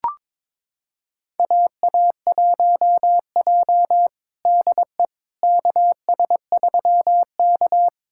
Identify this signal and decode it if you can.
Morse code, heard as a keyed tone, about 22 wpm, 700 Hz. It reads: AA1J DE KS3K
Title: USA Callsign DE Callsign - 4 Characters - 22wpm